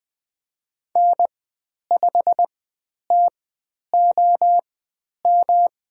Morse code, heard as a keyed tone, about 20 wpm, spaced as for 10 wpm, 700 Hz. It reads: N5TOM
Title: 109 Lesson 18g - Letter m - Call Signs 20wpm